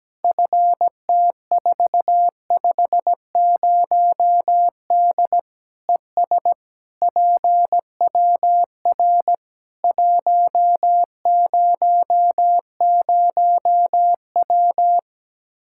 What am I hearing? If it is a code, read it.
Morse code, heard as a keyed tone, about 17 wpm, 700 Hz. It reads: FT450D ES PWR 100W
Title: Practice QSO - Exchange 2 - Info - 17wpm